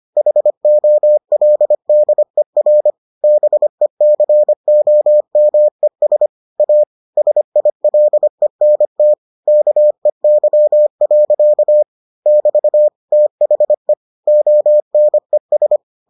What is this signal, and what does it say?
Morse code, heard as a keyed tone, about 25 wpm, 600 Hz. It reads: HOLDER BECOMES A SILENT KEY. <BT> THE ONES